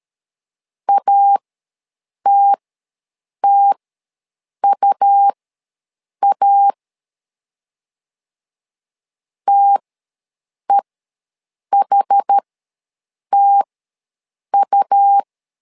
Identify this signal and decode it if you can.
Morse code, heard as a keyed tone, about 13 wpm, spaced as for 7 wpm, 800 Hz. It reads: ATTUA TEHTU
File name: LEZIONI / 2 UHDN / 2 UH ws 12wpm 1730